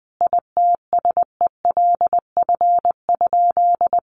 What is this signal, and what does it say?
Morse code, heard as a keyed tone, about 20 wpm, 700 Hz. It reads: ITSELF?
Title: Sentences from Top 800 Words 20wpm